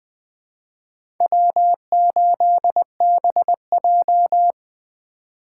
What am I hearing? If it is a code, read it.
Morse code, heard as a keyed tone, about 20 wpm, 700 Hz. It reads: W8BJ
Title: USA Call Signs - 4 Characters - 3x Character Spacing - 20wpm